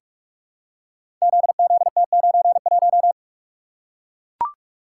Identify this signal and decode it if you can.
Morse code, heard as a keyed tone, about 44 wpm, 700 Hz. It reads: ZZT91
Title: Code Groups - 5 Letter:Numbers - 45wpm